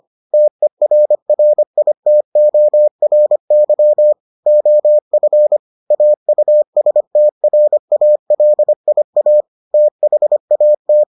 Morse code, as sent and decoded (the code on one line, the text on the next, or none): - . .-. .-. .. - --- .-. -.-- / --- ..-. / .- ..- ... - .-. .- .-.. .. .- / - .... .- -
TERRITORY OF AUSTRALIA THAT